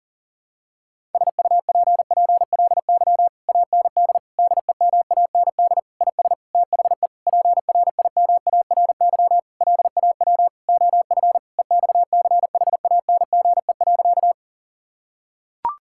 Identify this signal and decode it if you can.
Morse code, heard as a keyed tone, about 40 wpm, 700 Hz. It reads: SUPPLY AND DEMAND IS THE PRIMARY LAW OF EXCHANGE.
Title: Sentences from Top 1000 Words - Review Part 6 - 40wpm